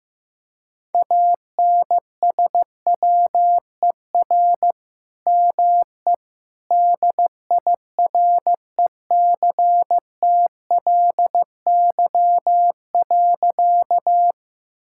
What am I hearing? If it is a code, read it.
Morse code, heard as a keyed tone, about 15 wpm, 700 Hz. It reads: ANSWER ME DIRECTLY.
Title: Sentences from Top 1000 Words - Review Part 5 - 15wpm